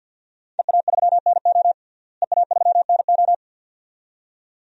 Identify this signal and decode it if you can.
Morse code, heard as a keyed tone, about 46 wpm, 700 Hz. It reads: EA3NO EA3NO
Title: DXCC-100 Call Signs - 5 Characters - Repeated 2x - 50wpm